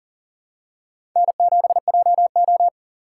Morse code, heard as a keyed tone, about 40 wpm, 700 Hz. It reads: N7JO
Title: USA Call Signs - 4 Characters - 6x Character Spacing - 40wpm